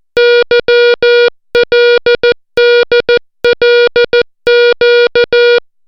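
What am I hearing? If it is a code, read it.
Morse code, heard as a keyed tone, about 14 wpm, 480 Hz. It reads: YLDLQ
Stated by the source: Morse alphabet lesson, 60 characters per minute.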